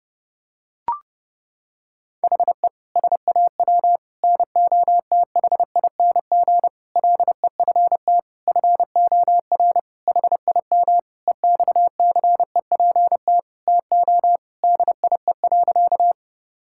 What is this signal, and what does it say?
Morse code, heard as a keyed tone, about 30 wpm, 700 Hz. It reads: HE SAW NOTHING LEFT FOR HIM EXCEPT TO DIE.
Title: Sentences from Top 1600 Words 30wpm